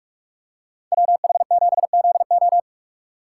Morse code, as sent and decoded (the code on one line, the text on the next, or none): .-- .... --... --.. ---
WH7ZO